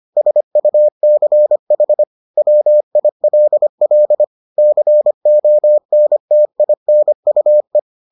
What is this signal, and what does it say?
Morse code, heard as a keyed tone, about 25 wpm, 600 Hz. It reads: SUCH WILL CONTINUE